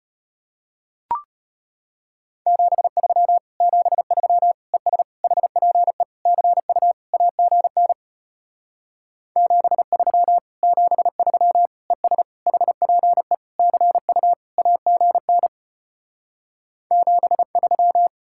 Morse code, heard as a keyed tone, about 33 wpm, 700 Hz. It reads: 73 73 ES HPE CU AGN 73 73 ES HPE CU AGN 73